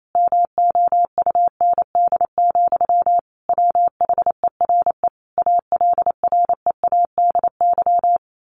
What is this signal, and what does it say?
Morse code, heard as a keyed tone, about 28 wpm, 700 Hz. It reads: MOUND, WHERE ALREADY
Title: A Sound Of Thunder 28wpm